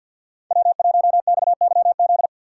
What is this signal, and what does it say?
Morse code, heard as a keyed tone, about 47 wpm, 700 Hz. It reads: W1XYZ
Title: USA Call Signs - 5 Characters - 2x Character Spacing - 50wpm